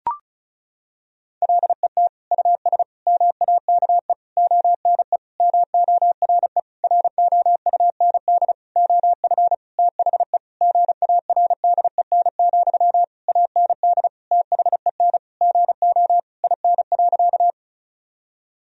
Morse code, heard as a keyed tone, about 36 wpm, 700 Hz. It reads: LET US MAKE ONE MORE ROUND OF THE GARDEN, AND THEN GO IN.
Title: Sentences from Top 1000 Words - Review Part 4 - 35wpm